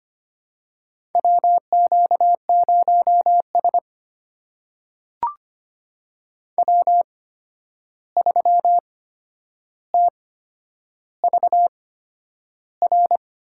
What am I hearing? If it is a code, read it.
Morse code, heard as a keyed tone, about 25 wpm, spaced as for 10 wpm, 700 Hz. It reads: WQ0S W3TVR
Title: USA Call Signs - 8x Character Spacing - 25wpm